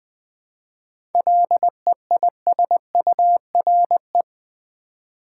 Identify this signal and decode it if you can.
Morse code, heard as a keyed tone, about 20 wpm, 690 Hz. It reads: LEISURE
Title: 7 Letter Words - First 3 Letters Given - 20wpm